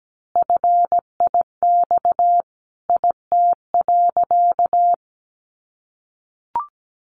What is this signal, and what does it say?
Morse code, heard as a keyed tone, about 17 wpm, 700 Hz. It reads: FIX IT.
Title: Sentences from Top 1400 Words 17wpm